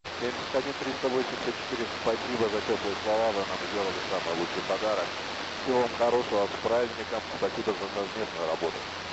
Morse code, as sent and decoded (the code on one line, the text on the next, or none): none